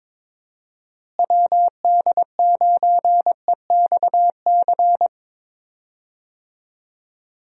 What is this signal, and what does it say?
Morse code, heard as a keyed tone, about 21 wpm, 700 Hz. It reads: WD9EXC
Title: USA Call Signs - 6 Characters - Repeated 3x - 22wpm